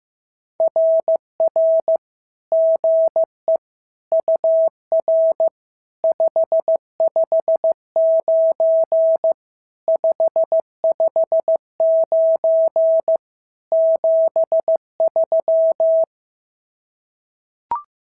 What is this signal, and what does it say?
Morse code, heard as a keyed tone, about 15 wpm, 650 Hz. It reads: RR GE UR 559 559 73
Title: SOTA QSO Practice 15wpm